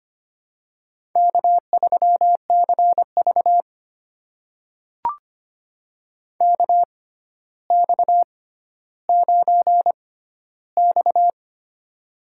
Morse code, heard as a keyed tone, about 25 wpm, spaced as for 13 wpm, 700 Hz. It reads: K3CV KX9X
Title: USA Call Signs - 4 Characters - 6x Character Spacing - 25wpm